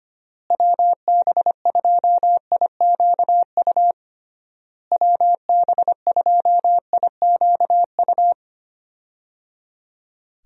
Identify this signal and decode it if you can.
Morse code, heard as a keyed tone, about 25 wpm, 700 Hz. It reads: WB2IQU WB2IQU
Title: USA Call Signs - 6 Characters - Repeated 3x - 25wpm